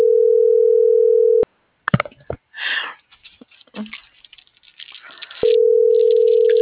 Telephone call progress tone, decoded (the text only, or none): ringback tone